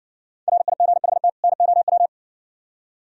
Morse code, heard as a keyed tone, about 57 wpm, 700 Hz. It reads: RIGHT NOW